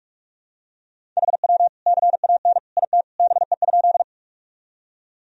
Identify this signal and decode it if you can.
Morse code, heard as a keyed tone, about 46 wpm, 700 Hz. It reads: HW CAN IT BE?
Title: Sentences from Top 100 Words and Abbreviations 45wpm